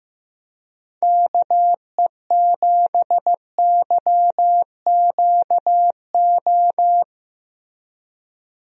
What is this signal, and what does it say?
Morse code, heard as a keyed tone, about 16 wpm, 700 Hz. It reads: KE7YQO
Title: USA Call Signs - 6 Characters - Repeated 2x - 15wpm